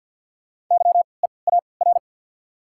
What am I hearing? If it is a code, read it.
Morse code, heard as a keyed tone, about 52 wpm, 690 Hz. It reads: Y E A R
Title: 3x Character Spacing - Top 200 Words - 50wpm